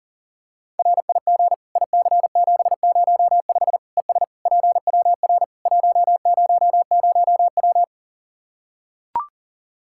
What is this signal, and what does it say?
Morse code, heard as a keyed tone, about 40 wpm, 700 Hz. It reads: RIG IC705 ES PWR 100W
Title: Practice QSO - Exchange 2 - Info - 40wpm